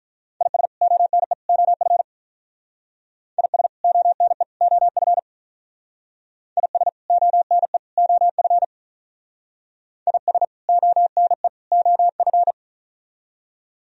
Morse code, heard as a keyed tone, about 44 wpm, 700 Hz. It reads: IS ONE OF IS ONE OF IS ONE OF IS ONE OF
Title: Speed-Racing - Sets of 3 Words 35wpm